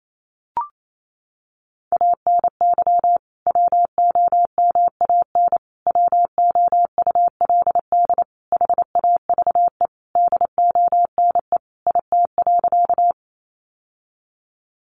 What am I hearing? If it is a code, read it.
Morse code, heard as a keyed tone, about 28 wpm, 700 Hz. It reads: ANY WOMAN WOULD HAVE DONE IT.
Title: Sentences from Top 1100 Words 28wpm